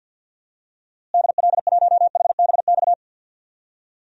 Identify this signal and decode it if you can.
Morse code, heard as a keyed tone, about 52 wpm, 700 Hz. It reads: DL1HBX
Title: DXCC-100 Call Signs - 6 Characters - Repeated 2x - 50wpm